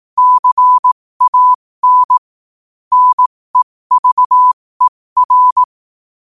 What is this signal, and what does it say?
Morse code, heard as a keyed tone, about 18 wpm, spaced as for 15 wpm, 1000 Hz. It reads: CAN NEVER
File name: qotdfiles / 15wpm